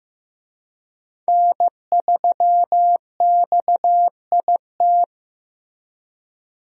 Morse code, heard as a keyed tone, about 15 wpm, 700 Hz. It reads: N3XIT
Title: USA Call Signs - 5 Characters - Repeated 3x - 15wpm